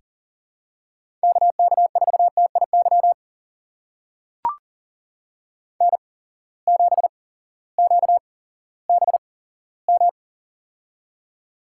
Morse code, heard as a keyed tone, about 40 wpm, spaced as for 18 wpm, 700 Hz. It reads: KK4TIY N7QBM
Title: USA Call Signs - 8x Character Spacing - 40wpm